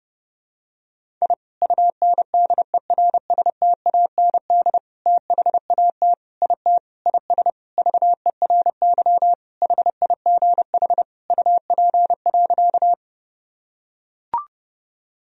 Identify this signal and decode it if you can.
Morse code, heard as a keyed tone, about 29 wpm, 700 Hz. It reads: I UNDERSTAND THAT IT IS VERY HIGH UP.
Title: Sentences from Top 1000 Words - Review Part 1 - 30wpm